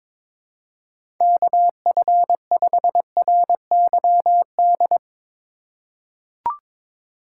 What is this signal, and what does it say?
Morse code, heard as a keyed tone, about 22 wpm, 700 Hz. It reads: KF5RYD
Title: USA Call Signs - 6 Characters - 3x Character Spacing - 22wpm